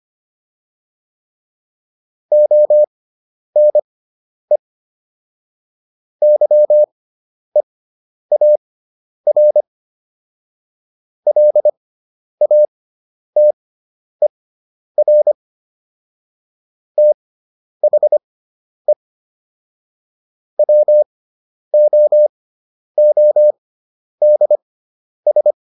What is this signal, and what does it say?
Morse code, heard as a keyed tone, about 25 wpm, spaced as for 10 wpm, 600 Hz. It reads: ONE YEAR LATER THE WOODS